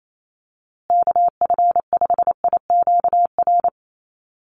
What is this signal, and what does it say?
Morse code, heard as a keyed tone, about 28 wpm, 700 Hz. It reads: KF5IQR
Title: USA Call Signs Repeated 3x 28wpm